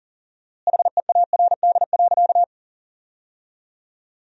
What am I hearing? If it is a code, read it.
Morse code, heard as a keyed tone, about 37 wpm, 700 Hz. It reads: HEARD.
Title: 4x Word-Spacing - Sentences from Top 400 Words - 40wpm